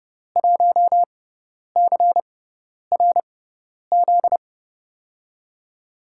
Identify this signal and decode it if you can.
Morse code, heard as a keyed tone, about 30 wpm, spaced as for 10 wpm, 700 Hz. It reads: 1CRZ
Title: USA Call Signs - 5 Characters - 6x Character Spacing - 30wpm